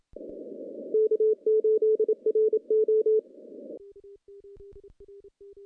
Morse code, heard as a keyed tone, about 27 wpm, 440 Hz. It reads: K8RO
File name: contests / CW-SS-k5tr- -0400